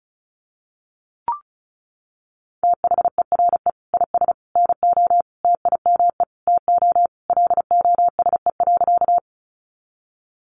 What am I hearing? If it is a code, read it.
Morse code, heard as a keyed tone, about 34 wpm, 700 Hz. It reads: THERE IS NO TIME TO LOSE.
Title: Sentences from Top 1100 Words 35wpm